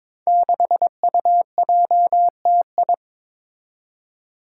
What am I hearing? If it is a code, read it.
Morse code, heard as a keyed tone, about 22 wpm, 700 Hz. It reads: 6UJTI
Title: Five Challenging Letter-Number Combinations 22wpm